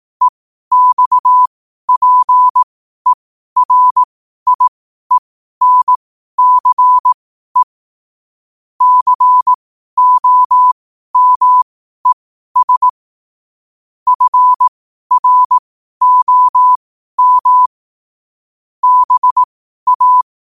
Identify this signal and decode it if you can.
Morse code, heard as a keyed tone, about 18 wpm, spaced as for 12 wpm, 1000 Hz. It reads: EXPERIENCE COMES FROM BA
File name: qotdfiles / 12wpm